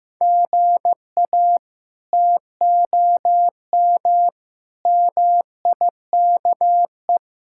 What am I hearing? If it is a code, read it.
Morse code, heard as a keyed tone, about 15 wpm, 700 Hz. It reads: GA TOM MIKE
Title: K1USN SST QSO Practice - 15wpm